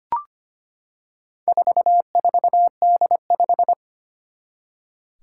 Morse code, heard as a keyed tone, about 25 wpm, 700 Hz. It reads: 44D5
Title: Four Challenging Letter-Number Combinations 25wpm